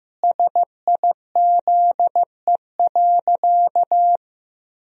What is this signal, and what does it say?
Morse code, heard as a keyed tone, about 15 wpm, 700 Hz. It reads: SIZE.
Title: Sentences from Top 1000 Words - Review Part 3 - 15wpm